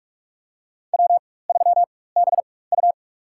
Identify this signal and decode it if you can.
Morse code, heard as a keyed tone, about 46 wpm, spaced as for 21 wpm, 700 Hz. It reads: W3BU
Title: USA Call Signs - 4 Characters - 4x Character Spacing - 45wpm